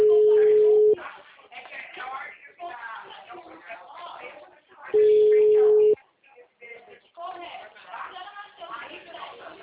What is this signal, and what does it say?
Ringback tone (Europe, 425 Hz), with speech.